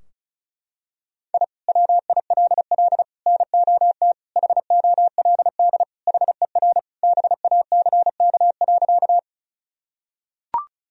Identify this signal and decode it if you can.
Morse code, heard as a keyed tone, about 36 wpm, 700 Hz. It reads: I WILL NOT HOLD HER BACK.
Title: Sentences from Top 1000 Words - Review Part 2 - 35wpm